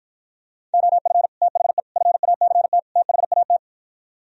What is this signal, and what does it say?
Morse code, heard as a keyed tone, about 56 wpm, 700 Hz. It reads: OF THE FACT THAT